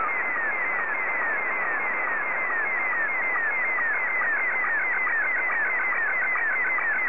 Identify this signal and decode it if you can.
SSTV (slow-scan television), Scottie 1.
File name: JA3OEN